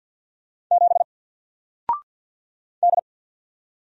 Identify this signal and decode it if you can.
Morse code, heard as a keyed tone, about 50 wpm, 700 Hz. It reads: 7 D
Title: Single Letter-Number Alt 50wpm